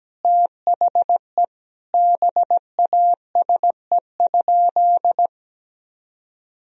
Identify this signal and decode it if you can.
Morse code, heard as a keyed tone, about 17 wpm, 700 Hz. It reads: THE BASE?
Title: Sentences from Top 1000 Words - Review Part 1 - 17wpm